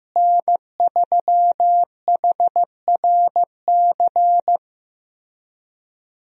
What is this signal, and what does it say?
Morse code, heard as a keyed tone, about 15 wpm, 700 Hz. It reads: N3HRC
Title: USA Call Signs - 5 Characters - Repeated 3x - 15wpm